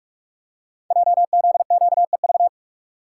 Morse code, heard as a keyed tone, about 44 wpm, 690 Hz. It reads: JZQEV
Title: Code Groups - 5 Letters - 45wpm